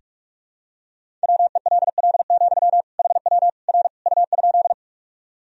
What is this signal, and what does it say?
Morse code, heard as a keyed tone, about 46 wpm, 700 Hz. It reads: WELL, HW R U?
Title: Sentences from Top 200 Words and Abbreviations 45wpm